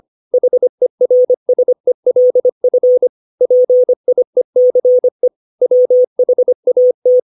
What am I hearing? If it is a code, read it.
Morse code, heard as a keyed tone, about 25 wpm, 500 Hz. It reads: HERSELF PIECE WHAT
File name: 632 words / 632 Words 25WPM